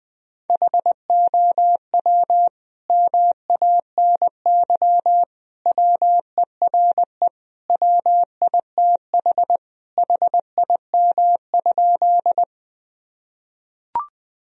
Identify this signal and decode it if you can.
Morse code, heard as a keyed tone, about 21 wpm, 700 Hz. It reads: HOW MANY WERE WITH HIM?
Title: Sentences from Top 1000 Words - Review Part 1 - 20wpm